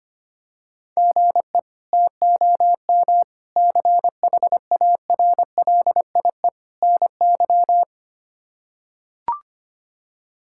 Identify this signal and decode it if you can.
Morse code, heard as a keyed tone, about 25 wpm, 700 Hz. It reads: GE TOM CHARLIE NY
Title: K1USN SST QSO Practice - 25wpm